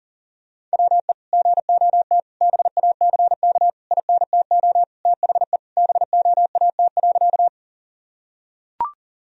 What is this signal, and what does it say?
Morse code, heard as a keyed tone, about 40 wpm, 700 Hz. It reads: WE GOT BACK INTO THE BOAT.
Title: Sentences from Top 1000 Words - Review Part 3 - 40wpm